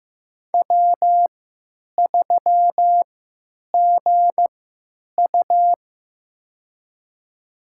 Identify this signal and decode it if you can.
Morse code, heard as a keyed tone, about 15 wpm, spaced as for 9 wpm, 700 Hz. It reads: W3GU
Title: USA Call Signs - 4 Characters - 3x Character Spacing - 15wpm